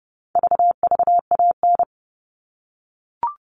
Morse code, heard as a keyed tone, about 30 wpm, 700 Hz. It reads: VVAN
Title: All Cut Numbers - 4-Digit 30wpm